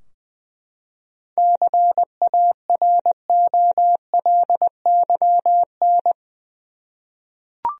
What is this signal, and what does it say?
Morse code, heard as a keyed tone, about 20 wpm, 700 Hz. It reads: CAROLYN
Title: USA Common Names - 20wpm